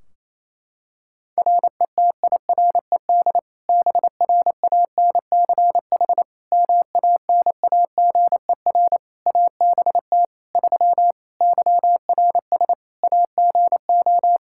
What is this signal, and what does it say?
Morse code, heard as a keyed tone, about 28 wpm, 700 Hz. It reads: RETIRED BRANCH MANAGER ABT 3 YRS AGO
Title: Practice QSO - Exchange 3 - More Info - 28wpm